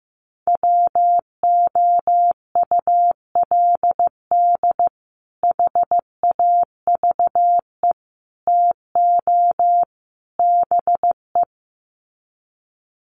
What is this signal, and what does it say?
Morse code, heard as a keyed tone, about 15 wpm, 700 Hz. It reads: WOULD HAVE TO BE
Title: Sets of 4 Words 15wpm